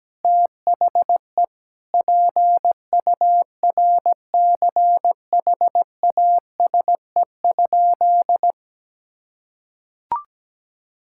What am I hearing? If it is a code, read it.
Morse code, heard as a keyed tone, about 17 wpm, 700 Hz. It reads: THE PURCHASE?
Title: Sentences from Top 1000 Words - Review Part 5 - 17wpm